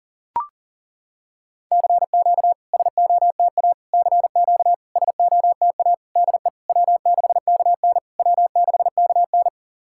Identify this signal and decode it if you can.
Morse code, heard as a keyed tone, about 40 wpm, 690 Hz. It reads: CQ SOTA CQ SOTA DE W6KN W6KN
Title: SOTA QSO Practice 40wpm